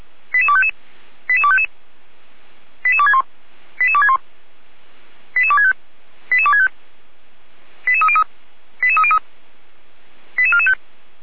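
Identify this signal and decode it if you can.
Five-tone selective call (ZVEI), heard as quick runs of five tones, tones of about 70 ms. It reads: 88255 88255 88261 88261 88265 88265 88303 88303 88445
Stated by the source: ZVEI five-tone call 88231 received on an SDR.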